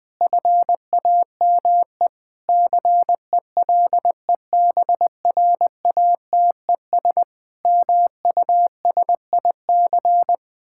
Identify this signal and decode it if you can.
Morse code, heard as a keyed tone, about 21 wpm, 700 Hz. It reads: FAME CELEBRATES MUSIC
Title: Sentences with 7-Word Phrases - 20wpm